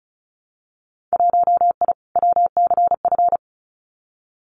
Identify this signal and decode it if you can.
Morse code, heard as a keyed tone, about 34 wpm, 700 Hz. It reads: 1I WCF
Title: Winter Field Day Exchanges Encore 35wpm